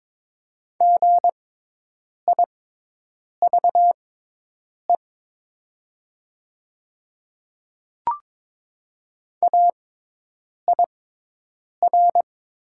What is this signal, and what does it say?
Morse code, heard as a keyed tone, about 22 wpm, spaced as for 8 wpm, 700 Hz. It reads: GIVE AIR
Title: 6x Character Spacing - Top 200 Words - 22wpm